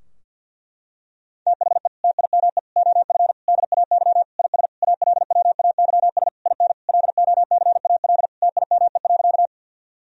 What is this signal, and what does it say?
Morse code, heard as a keyed tone, about 52 wpm, 700 Hz. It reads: THE TIME OF DAY IS ALWAYS IN LOCAL TIME.
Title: Sentences from Top 1000 Words - Review Part 3 - 50wpm